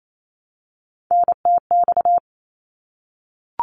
Morse code, heard as a keyed tone, about 28 wpm, 700 Hz. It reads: NTX
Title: ARRL Sections 28wpm